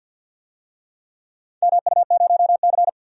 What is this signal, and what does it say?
Morse code, heard as a keyed tone, about 52 wpm, 700 Hz. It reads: MU0C
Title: DXCC-100 Call Signs - 4 Characters - 50wpm